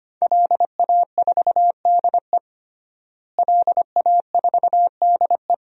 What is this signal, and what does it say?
Morse code, heard as a keyed tone, about 25 wpm, 700 Hz. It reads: LA4DE LA4DE
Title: DXCC-100 Call Signs - 5 Characters - Repeated 2x - 25wpm